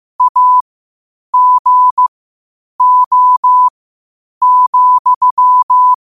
Morse code, heard as a keyed tone, about 15 wpm, spaced as for 8 wpm, 1000 Hz. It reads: AGO,
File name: qotdfiles / 8wpm